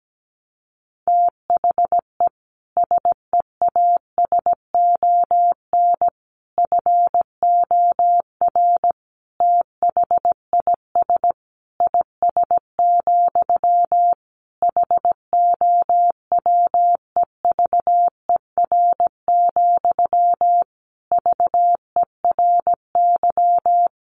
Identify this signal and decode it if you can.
Morse code, heard as a keyed tone, about 17 wpm, 700 Hz. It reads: THE SEASON FOR THIS IS, HOWEVER, VERY
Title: Sentences from Top 1400 Words 17wpm